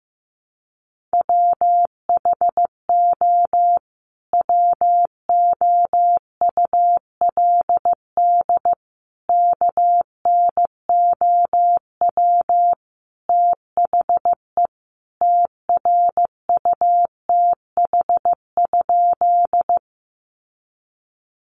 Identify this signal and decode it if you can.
Morse code, heard as a keyed tone, about 15 wpm, 700 Hz. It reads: WHO WOULD KNOW THE TRUTH?
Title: Sentences from Top 800 Words 15wpm